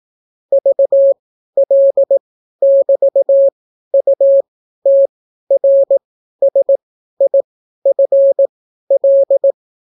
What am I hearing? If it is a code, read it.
Morse code, heard as a keyed tone, about 18 wpm, spaced as for 12 wpm, 550 Hz. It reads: VL<BT>UTRSIFL